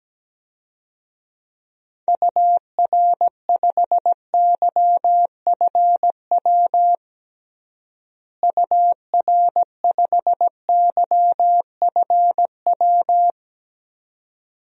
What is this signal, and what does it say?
Morse code, heard as a keyed tone, about 17 wpm, 700 Hz. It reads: UR5YFW UR5YFW
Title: DXCC-100 Call Signs - 6 Characters - Repeated 2x - 17wpm